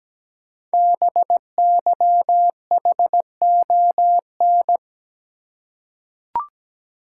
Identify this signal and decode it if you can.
Morse code, heard as a keyed tone, about 17 wpm, 700 Hz. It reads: BYHON
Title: Code Groups - 5 Letters - 17wpm